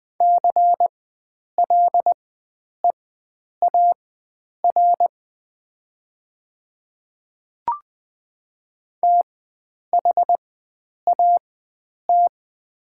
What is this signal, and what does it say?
Morse code, heard as a keyed tone, about 21 wpm, spaced as for 9 wpm, 700 Hz. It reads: CLEAR THAT
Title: 4x Character Spacing - Top 500 Words - 20wpm